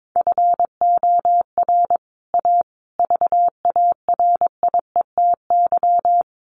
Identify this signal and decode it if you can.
Morse code, heard as a keyed tone, about 22 wpm, 700 Hz. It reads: FOR A VARIETY
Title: Sets of 5 Words Encore 22wpm